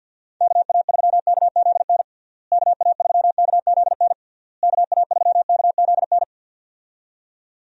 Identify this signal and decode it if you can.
Morse code, heard as a keyed tone, about 53 wpm, 700 Hz. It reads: KA3KZN KA3KZN KA3KZN
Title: USA Call Signs - 6 Characters - Repeated 3x - 50wpm